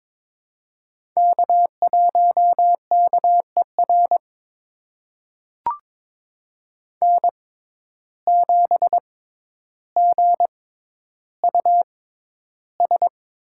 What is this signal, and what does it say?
Morse code, heard as a keyed tone, about 22 wpm, spaced as for 11 wpm, 700 Hz. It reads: K1KER N7GUS
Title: USA Call Signs - 5 Characters - 6x Character Spacing - 22wpm